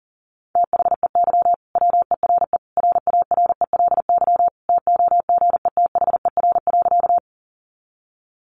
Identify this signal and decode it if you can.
Morse code, heard as a keyed tone, about 40 wpm, 700 Hz. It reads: THEY WERE RARELY TOGETHER.
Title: Sentences from Top 1400 Words 40wpm